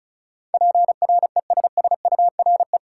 Morse code, heard as a keyed tone, about 36 wpm, 700 Hz. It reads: PRESSURE
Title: Sentences from Top 1000 Words - Review Part 6 - New Word Drill - 35wpm